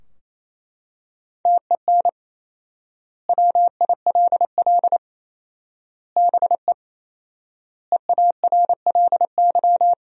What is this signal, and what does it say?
Morse code, heard as a keyed tone, about 28 wpm, 700 Hz. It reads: TEN WILL BE EARLY
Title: 4x Word-Spacing - Sentences from Top 400 Words - 28wpm